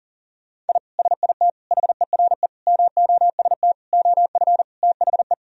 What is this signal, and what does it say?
Morse code, heard as a keyed tone, about 40 wpm, 700 Hz. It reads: I SIT HERE MOST OF THE
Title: Sentences from Top 1000 Words - Review Part 1 - 40wpm